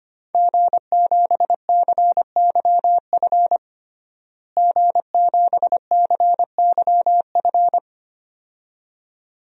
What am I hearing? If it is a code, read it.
Morse code, heard as a keyed tone, about 25 wpm, 700 Hz. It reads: G7CYF G7CYF
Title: DXCC-100 Call Signs - 5 Characters - Repeated 2x - 25wpm